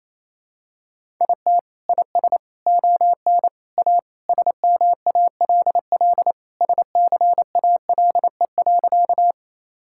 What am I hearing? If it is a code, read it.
Morse code, heard as a keyed tone, about 28 wpm, 700 Hz. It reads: IT IS ON A SMALL SCALE.
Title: Sentences from Top 1000 Words - Review Part 5 - 28wpm